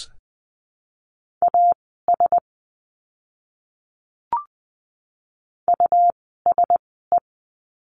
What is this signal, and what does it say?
Morse code, heard as a keyed tone, about 20 wpm, spaced as for 14 wpm, 700 Hz. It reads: AS USE
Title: 101 Lesson 17f - Letter u - Words 20wpm